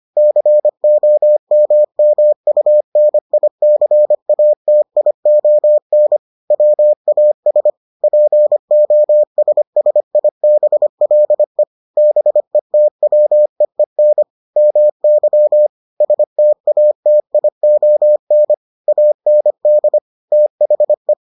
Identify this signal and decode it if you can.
Morse code, heard as a keyed tone, about 25 wpm, 600 Hz. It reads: COMMUNICATION WAS POSSIBLE BETWEEN MY STATION AND THE